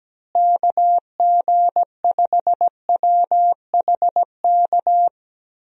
Morse code, heard as a keyed tone, about 17 wpm, 700 Hz. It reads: KG5WHK
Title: USA Call Signs - 6 Characters - Repeated 2x - 17wpm